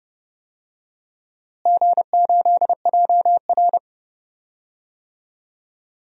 Morse code, heard as a keyed tone, about 30 wpm, 700 Hz. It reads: G8JR
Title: DXCC-100 Call Signs - 4 Characters - Repeated 2x - 30wpm